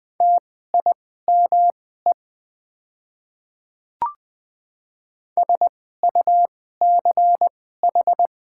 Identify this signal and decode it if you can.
Morse code, heard as a keyed tone, about 21 wpm, spaced as for 15 wpm, 690 Hz. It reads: TIME SUCH
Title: 2x Character Spacing - Top 200 Words - 20wpm